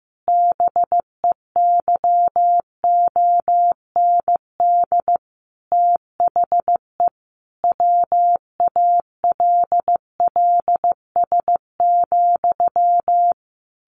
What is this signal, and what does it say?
Morse code, heard as a keyed tone, about 15 wpm, 700 Hz. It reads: BEYOND THE WALLS,
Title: A Sound Of Thunder 15wpm